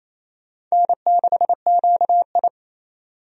Morse code, heard as a keyed tone, about 28 wpm, 690 Hz. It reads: N6QI
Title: USA Call Signs - 4 Characters - 28wpm